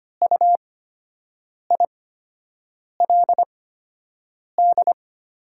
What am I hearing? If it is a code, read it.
Morse code, heard as a keyed tone, about 25 wpm, spaced as for 7 wpm, 700 Hz. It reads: UILD
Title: 8x Character Spacing - Top 200 Words - 25wpm